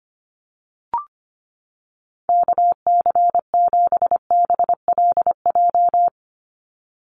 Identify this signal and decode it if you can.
Morse code, heard as a keyed tone, about 25 wpm, 700 Hz. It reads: KC7BLJ
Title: USA Call Signs 25wpm